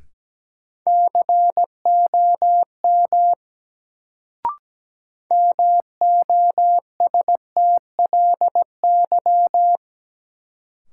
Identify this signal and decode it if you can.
Morse code, heard as a keyed tone, about 17 wpm, 700 Hz. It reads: COM MOSTLY
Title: Sentences from Top 1300 Words - New Word Drill Alt - 17wpm